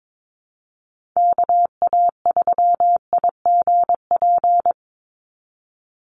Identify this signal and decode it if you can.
Morse code, heard as a keyed tone, about 22 wpm, 700 Hz. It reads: KA3IGP
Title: USA Call Signs 22wpm